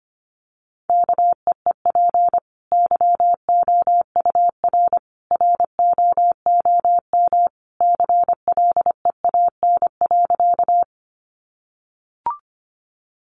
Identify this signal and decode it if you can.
Morse code, heard as a keyed tone, about 24 wpm, 700 Hz. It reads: KEEP YOUR ROOM CLEAN.
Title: Sentences from Top 1100 Words 25wpm